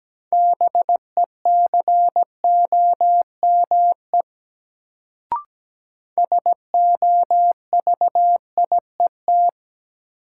Morse code, heard as a keyed tone, about 17 wpm, 700 Hz. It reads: BECOME SOVIET
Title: Sets of 5 Words Encore - New Word Drill - 17wpm